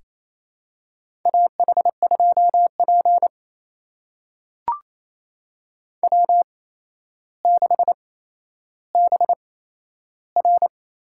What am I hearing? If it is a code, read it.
Morse code, heard as a keyed tone, about 27 wpm, spaced as for 12 wpm, 700 Hz. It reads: AH2P W6BR
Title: USA Call Signs - 4 Characters - 8x Character Spacing - 28wpm